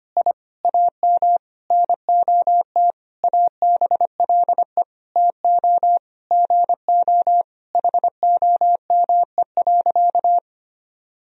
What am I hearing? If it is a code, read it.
Morse code, heard as a keyed tone, about 25 wpm, 700 Hz. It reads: I AM NOT ABLE TO GO HOME.
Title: Sentences from Top 1000 Words - Review Part 3 - 25wpm